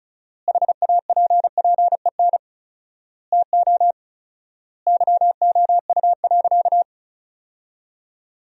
Morse code, heard as a keyed tone, about 34 wpm, 700 Hz. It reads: HAPPEN TO YOU.
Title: 4x Word-Spacing - Sentences from Top 500 Words - 35wpm